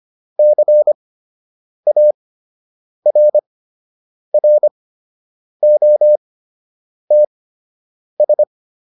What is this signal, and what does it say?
Morse code, heard as a keyed tone, about 25 wpm, spaced as for 8 wpm, 600 Hz. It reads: CARROTS